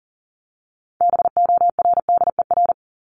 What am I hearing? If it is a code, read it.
Morse code, heard as a keyed tone, about 40 wpm, 700 Hz. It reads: BORDER
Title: Top 1751-2000 Words 40wpm